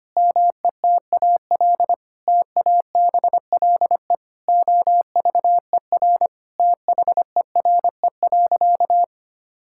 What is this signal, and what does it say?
Morse code, heard as a keyed tone, about 25 wpm, 700 Hz. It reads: METAL TABLE OVER THERE.
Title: Sentences from Top 1000 Words - Review Part 6 - 25wpm